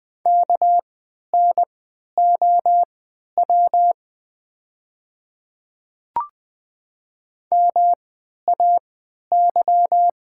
Morse code, heard as a keyed tone, about 20 wpm, spaced as for 11 wpm, 700 Hz. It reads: KNOW MAY
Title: 3x Character Spacing - Top 100 Words - 20wpm